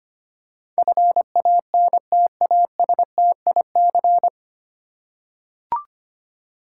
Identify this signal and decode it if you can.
Morse code, heard as a keyed tone, about 25 wpm, 700 Hz. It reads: FANTASTIC
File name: Nine Letter Words - 25wpm